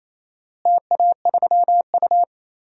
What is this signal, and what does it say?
Morse code, heard as a keyed tone, about 27 wpm, 700 Hz. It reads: TA3U
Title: DXCC-100 Call Signs - 4 Characters - Repeated 2x - 28wpm